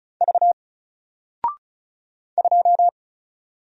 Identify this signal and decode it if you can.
Morse code, heard as a keyed tone, about 36 wpm, 700 Hz. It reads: V 2
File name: Single-Letter-Number-35wpm